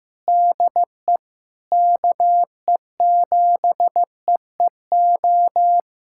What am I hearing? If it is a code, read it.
Morse code, heard as a keyed tone, about 15 wpm, 700 Hz. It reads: DE KE7EEO
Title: USA Callsign DE Callsign 15wpm